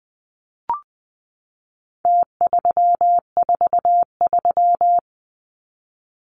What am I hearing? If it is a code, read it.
Morse code, heard as a keyed tone, about 20 wpm, 700 Hz. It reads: T343
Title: Numbers - Contest Style - 4-Digit 20wpm